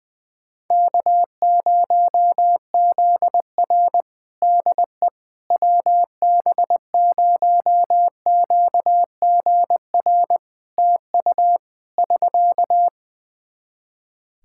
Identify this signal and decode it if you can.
Morse code, heard as a keyed tone, about 21 wpm, 700 Hz. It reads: K0ZR DE WB0QGR TU <SK>
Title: Practice QSO - Exchange 4 - Ending - 20wpm